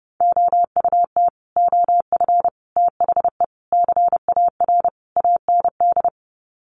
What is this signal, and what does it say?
Morse code, heard as a keyed tone, about 30 wpm, 700 Hz. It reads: OUT OF THE CAR AND
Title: Sets of 5 Words Encore 30wpm